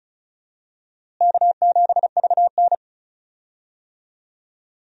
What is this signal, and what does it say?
Morse code, heard as a keyed tone, about 36 wpm, 700 Hz. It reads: K7VN
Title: USA Call Signs - 4 Characters - Repeated 3x - 35wpm